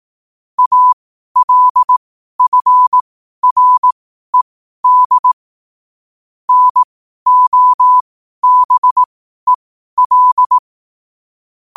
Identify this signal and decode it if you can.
Morse code, heard as a keyed tone, about 18 wpm, spaced as for 12 wpm, 1000 Hz. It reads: ALFRED NOBEL
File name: qotdfiles / 12wpm